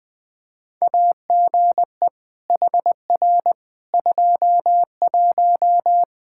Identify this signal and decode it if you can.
Morse code, heard as a keyed tone, about 21 wpm, 700 Hz. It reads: AGE HR 21
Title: K1USN SST QSO Practice 20wpm